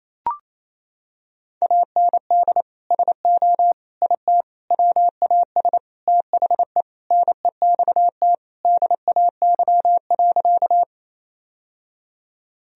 Morse code, heard as a keyed tone, about 28 wpm, 700 Hz. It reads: AND SO IT WAS THE NEXT DAY.
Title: Sentences from Top 1000 Words - Review Part 1 - 28wpm